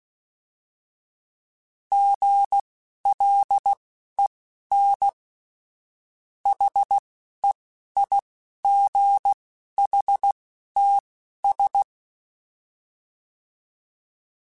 Morse code, heard as a keyed tone, about 16 wpm, spaced as for 11 wpm, 800 Hz. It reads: GLEN HEIGHTS
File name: fromSMs / FromSMs mp3s / Cities@10